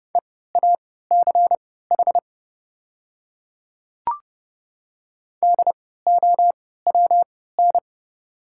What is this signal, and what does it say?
Morse code, heard as a keyed tone, about 30 wpm, spaced as for 17 wpm, 690 Hz. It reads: EACH DOWN
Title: 3x Character Spacing - Top 100 Words - 30wpm